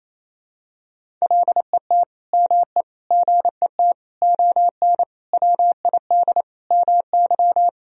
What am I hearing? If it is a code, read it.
Morse code, heard as a keyed tone, about 28 wpm, 700 Hz. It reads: LET ME GET ON WID MY
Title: Sentences from Top 300 Words and Abbreviations 28wpm